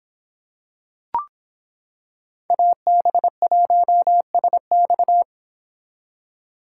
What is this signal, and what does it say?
Morse code, heard as a keyed tone, about 25 wpm, 700 Hz. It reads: AB1SX